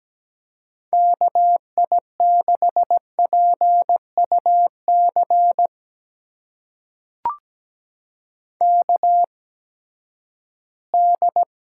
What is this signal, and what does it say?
Morse code, heard as a keyed tone, about 17 wpm, spaced as for 12 wpm, 700 Hz. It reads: KI6PUC KD
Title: USA Call Signs - 6 Characters - 8x Character Spacing - 17wpm